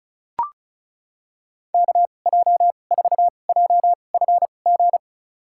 Morse code, heard as a keyed tone, about 34 wpm, spaced as for 25 wpm, 700 Hz. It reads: KJ4JFG
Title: USA Call Signs - 2x Character Spacing - 35wpm